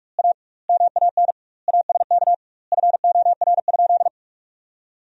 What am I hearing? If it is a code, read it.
Morse code, heard as a keyed tone, about 42 wpm, 700 Hz. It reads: A MAN ASK FOR?
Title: 2x Word-Spacing - Sentences from Top 200 Words - 45wpm